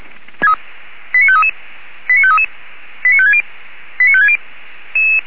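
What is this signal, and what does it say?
Five-tone selective call (ZVEI), heard as quick runs of five tones, tones of about 80 ms. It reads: 89533 89533 89560 89560
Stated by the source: ZVEI five-tone selective call received on an SDR.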